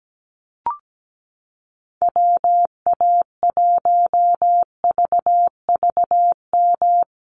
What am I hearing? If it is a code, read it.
Morse code, heard as a keyed tone, about 17 wpm, 700 Hz. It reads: WA1VVM
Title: USA Call Signs Repeated 2x 17wpm